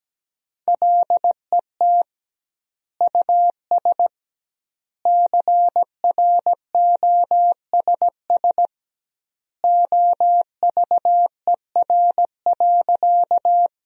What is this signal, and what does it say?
Morse code, heard as a keyed tone, about 17 wpm, 700 Hz. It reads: LET US CROSS OVER.
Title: 2x Word-Spacing - Sentences from Top 300 Words - 17wpm